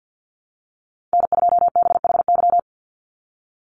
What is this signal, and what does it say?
Morse code, heard as a keyed tone, about 48 wpm, 700 Hz. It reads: N2BHY
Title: USA Call Signs 50wpm